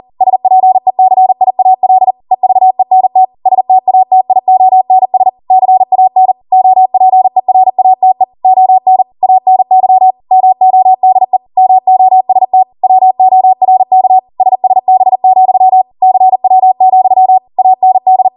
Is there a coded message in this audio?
Yes: Morse code from 0.2 to 18.3 s, heard as a keyed tone, about 40 wpm, 750 Hz. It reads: SPECIAL EVENT STATIONS CAN OPERATE ON ANY MODE MOST WORK SSB, CW, AND